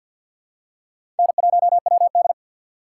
Morse code, heard as a keyed tone, about 47 wpm, 700 Hz. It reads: N1WD
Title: USA Call Signs - 4x Character Spacing - 50wpm